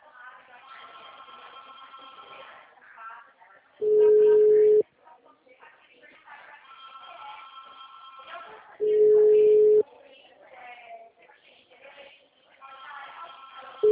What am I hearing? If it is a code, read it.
Ringback tone (Europe, 425 Hz), with speech.
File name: out- -2026- - - .77185